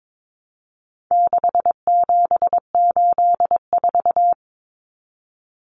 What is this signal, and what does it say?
Morse code, heard as a keyed tone, about 22 wpm, 700 Hz. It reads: 6784
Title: Common Cut Numbers - 4-Digit 22wpm